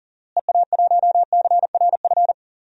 Morse code, heard as a keyed tone, about 40 wpm, 700 Hz. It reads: EA1CRF
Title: DXCC-100 Call Signs - 6 Characters - Repeated 2x - 40wpm